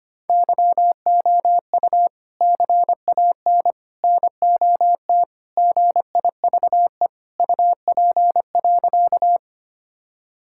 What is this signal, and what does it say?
Morse code, heard as a keyed tone, about 25 wpm, 700 Hz. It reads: YOU CAN NOT GIVE UP.
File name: practice / 25wpm / Sentences with 3-Word Phrases - 25wpm